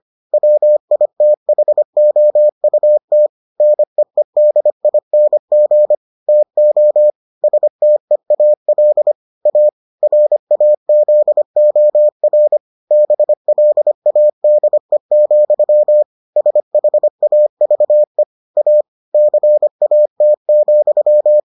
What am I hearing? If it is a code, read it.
Morse code, heard as a keyed tone, about 25 wpm, 600 Hz. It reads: WITHOUT NEEDING TO STEAL A RAZOR BLADE, SHAVE A CAT,